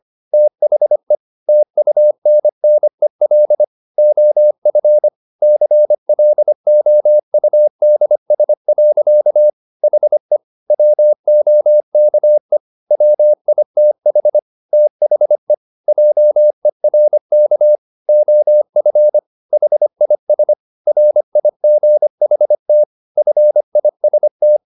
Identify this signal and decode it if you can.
Morse code, heard as a keyed tone, about 25 wpm, 600 Hz. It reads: THE TUNNEL OF CLOUDS. HE WOKE WITH THE JERK OF HIS RIGHT FIST